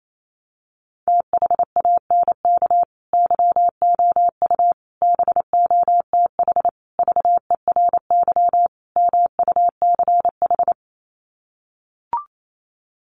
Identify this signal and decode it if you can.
Morse code, heard as a keyed tone, about 28 wpm, 700 Hz. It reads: THANK YOU BOTH VERY MUCH
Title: Sets of 5 Words 28wpm